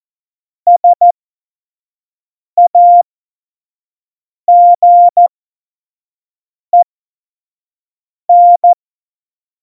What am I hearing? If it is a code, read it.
Morse code, heard as a keyed tone, about 14 wpm, spaced as for 5 wpm, 700 Hz. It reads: SAGEN
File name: Morsekurs / Witze / Witz 26. November 2019, 25 F70